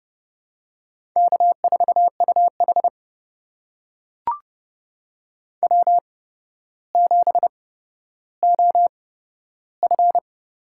Morse code, heard as a keyed tone, about 30 wpm, spaced as for 13 wpm, 700 Hz. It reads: K4UH W7OF
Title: USA Call Signs - 4 Characters - 8x Character Spacing - 30wpm